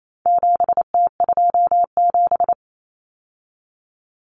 Morse code, heard as a keyed tone, about 28 wpm, 700 Hz. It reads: 7T27
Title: Common Cut Numbers - 4-Digit 28wpm